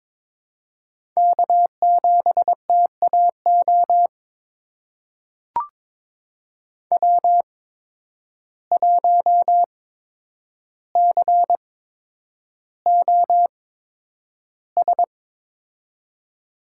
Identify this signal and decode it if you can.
Morse code, heard as a keyed tone, about 22 wpm, spaced as for 9 wpm, 700 Hz. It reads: K7TAO W1COS
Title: USA Call Signs - 5 Characters - 8x Character Spacing - 22wpm